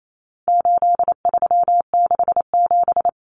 Morse code, heard as a keyed tone, about 28 wpm, 700 Hz. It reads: 8367
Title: Common Cut Numbers - 4-Digit 28wpm